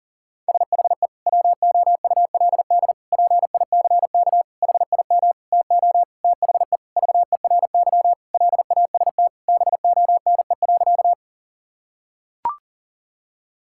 Morse code, heard as a keyed tone, about 37 wpm, 700 Hz. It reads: SHE WOULD PICK HIM TO THE VERY LAST BONE.
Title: Sentences from Top 1600 Words 40wpm